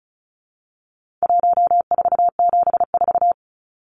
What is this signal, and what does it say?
Morse code, heard as a keyed tone, about 34 wpm, 700 Hz. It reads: 1474
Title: Numbers - 4-Digit 35wpm